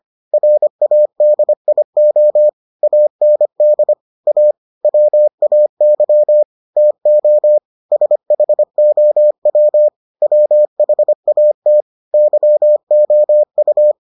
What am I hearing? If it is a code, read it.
Morse code, heard as a keyed tone, about 25 wpm, 600 Hz. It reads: RADIO AND A WAY TO SHOW WHAT YOU